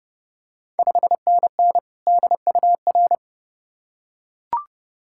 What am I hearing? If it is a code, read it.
Morse code, heard as a keyed tone, about 32 wpm, 700 Hz. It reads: 5NN DUR
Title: Ontario QSO Party Exchange 30wpm